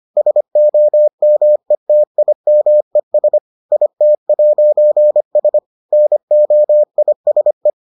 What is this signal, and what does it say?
Morse code, heard as a keyed tone, about 25 wpm, 600 Hz. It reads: SOMETIMES IT'S NOISE